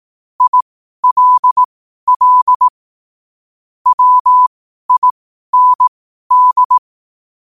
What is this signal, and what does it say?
Morse code, heard as a keyed tone, about 18 wpm, spaced as for 12 wpm, 1000 Hz. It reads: ILL WIND
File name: qotdfiles / 12wpm